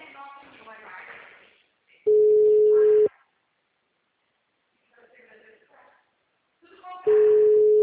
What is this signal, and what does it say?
Ringback tone (Europe, 425 Hz), with speech.
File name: out- -2027- - - .376